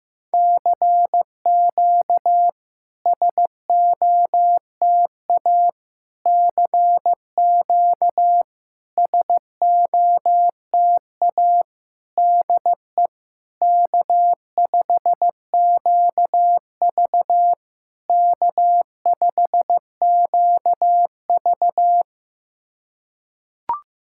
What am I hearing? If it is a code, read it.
Morse code, heard as a keyed tone, about 15 wpm, 700 Hz. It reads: CQ SOTA CQ SOTA DE K5QV K5QV
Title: SOTA QSO Practice 15wpm